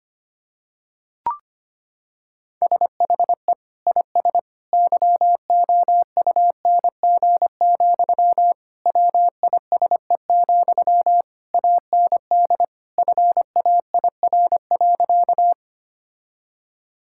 Morse code, heard as a keyed tone, about 25 wpm, 700 Hz. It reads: SHE IS YOUNG, WISE, AND FAIR.
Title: Sentences from Top 1600 Words 25wpm